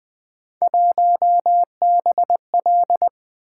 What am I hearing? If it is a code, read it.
Morse code, heard as a keyed tone, about 20 wpm, 700 Hz. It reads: 1BL
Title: Three Letter-Number Combinations - 20wpm